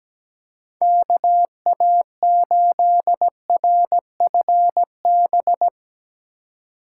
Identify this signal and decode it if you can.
Morse code, heard as a keyed tone, about 22 wpm, 700 Hz. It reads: KA8RFB